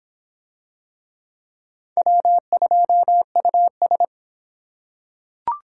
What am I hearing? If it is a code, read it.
Morse code, heard as a keyed tone, about 24 wpm, 700 Hz. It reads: W2US